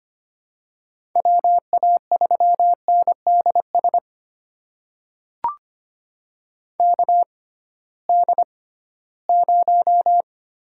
Morse code, heard as a keyed tone, about 25 wpm, spaced as for 16 wpm, 700 Hz. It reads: WA3NDS KD0
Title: USA Call Signs - 6 Characters - 6x Character Spacing - 25wpm